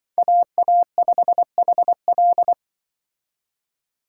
Morse code, heard as a keyed tone, about 24 wpm, 700 Hz. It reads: AA5HL
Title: Speed Racing - USA 2x2 Call Signs 20wpm